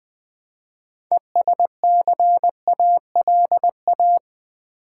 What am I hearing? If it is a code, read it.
Morse code, heard as a keyed tone, about 20 wpm, 700 Hz. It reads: ESCALA
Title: Build A Word - 3 Letters At A Time - 8-Letter Words - 20wpm